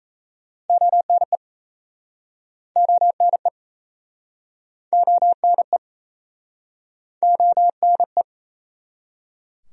Morse code, heard as a keyed tone, about 36 wpm, 700 Hz. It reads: ONE ONE ONE ONE